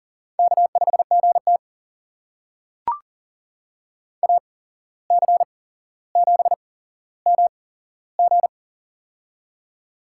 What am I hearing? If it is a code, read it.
Morse code, heard as a keyed tone, about 42 wpm, spaced as for 16 wpm, 700 Hz. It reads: K5GT AC7MG
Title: USA Call Signs - 8x Character Spacing - 40wpm